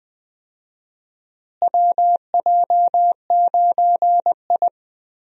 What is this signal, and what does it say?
Morse code, heard as a keyed tone, about 21 wpm, 700 Hz. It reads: WJ9I
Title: USA Call Signs - 4 Characters - Repeated 3x - 20wpm